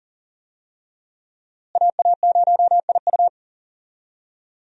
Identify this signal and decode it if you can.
Morse code, heard as a keyed tone, about 38 wpm, 700 Hz. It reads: AA0IU
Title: USA Call Signs - 5 Characters - Repeated 2x - 40wpm